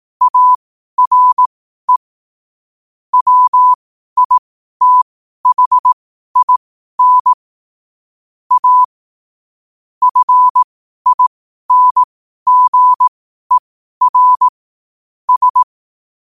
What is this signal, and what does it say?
Morse code, heard as a keyed tone, about 18 wpm, spaced as for 12 wpm, 1000 Hz. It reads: ARE WITHIN A FINGER S
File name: qotdfiles / 12wpm